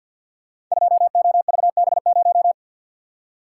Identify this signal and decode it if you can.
Morse code, heard as a keyed tone, about 48 wpm, 700 Hz. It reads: 2OVB0
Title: Code Groups - 5 Letter:Numbers - 50wpm